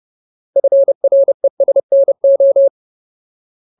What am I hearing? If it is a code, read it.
Morse code, heard as a keyed tone, about 30 wpm, 550 Hz. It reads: FRESNO
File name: Ultimate Word List / 30WPM Ultimate